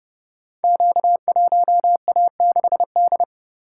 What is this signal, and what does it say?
Morse code, heard as a keyed tone, about 31 wpm, 700 Hz. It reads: Q1A6D
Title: Code Groups - 5 Letter:Numbers - 30wpm